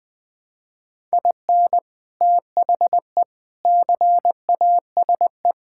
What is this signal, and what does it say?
Morse code, heard as a keyed tone, about 20 wpm, 700 Hz. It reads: IN THE CASE
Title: Build A Phrase - 3-Word Phrases - 20wpm